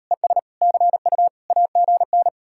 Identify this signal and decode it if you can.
Morse code, heard as a keyed tone, about 38 wpm, 700 Hz. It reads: ES CU AGN